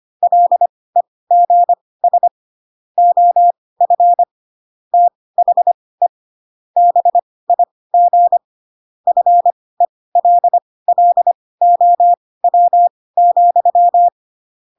Morse code, heard as a keyed tone, about 25 wpm, spaced as for 18 wpm, 700 Hz. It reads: LEGS OF THE BIG FELLOW,
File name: Pinocchio ch3 12 25 ik1hge